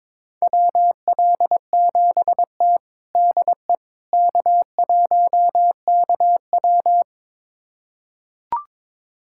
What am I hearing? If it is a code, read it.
Morse code, heard as a keyed tone, about 22 wpm, 700 Hz. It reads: WL7T DE K1KW
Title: USA Callsign DE Callsign - 4 Characters - 22wpm